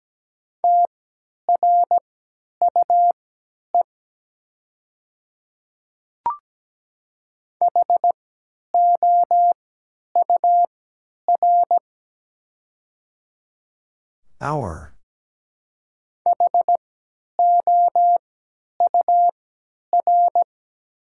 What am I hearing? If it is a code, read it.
Morse code, heard as a keyed tone, about 17 wpm, spaced as for 10 wpm, 700 Hz. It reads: TRUE HOUR HOUR
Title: 3x Character Spacing - Top 500 Words - 17wpm